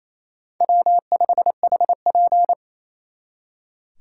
Morse code, heard as a keyed tone, about 35 wpm, 700 Hz. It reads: W5HP